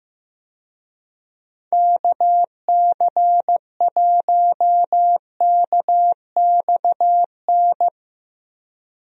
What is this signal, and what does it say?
Morse code, heard as a keyed tone, about 15 wpm, 700 Hz. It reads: KC1KXN
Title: USA Call Signs - 6 Characters - Repeated 2x - 15wpm